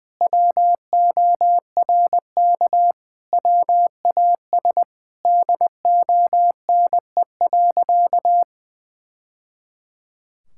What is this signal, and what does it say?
Morse code, heard as a keyed tone, about 19 wpm, 700 Hz. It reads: WORK WAS DONE.
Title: Sentences from Top 1000 Words - Review Part 5 - 20wpm